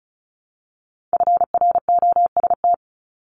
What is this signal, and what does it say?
Morse code, heard as a keyed tone, about 36 wpm, 690 Hz. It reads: FROST
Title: Multi-letter Phonemes 35wpm